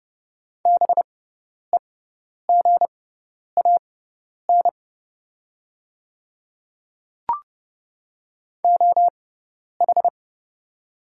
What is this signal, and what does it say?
Morse code, heard as a keyed tone, about 30 wpm, spaced as for 10 wpm, 700 Hz. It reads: BEGAN OH
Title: 6x Character Spacing - Top 500 Words - 30wpm